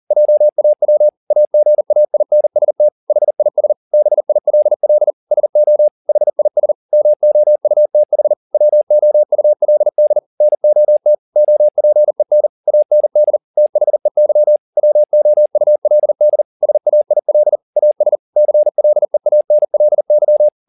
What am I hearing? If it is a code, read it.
Morse code, heard as a keyed tone, about 40 wpm, 600 Hz. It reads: JAW AGAINST HIS BILL SO HIS MOUTH WOULD NOT OPEN AND THEY WOULD SAIL AS CLEANLY
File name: the old man and the sea 2 40wpm